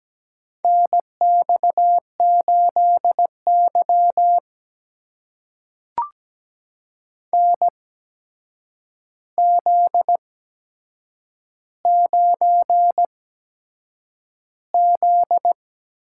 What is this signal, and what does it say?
Morse code, heard as a keyed tone, about 17 wpm, spaced as for 7 wpm, 700 Hz. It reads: NX8Y NZ9Z
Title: USA Call Signs - 4 Characters - 8x Character Spacing - 17wpm